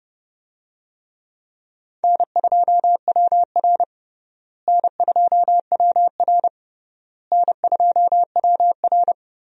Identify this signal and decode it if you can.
Morse code, heard as a keyed tone, about 30 wpm, 700 Hz. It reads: N2WR N2WR N2WR
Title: USA Call Signs - 4 Characters - Repeated 3x - 30wpm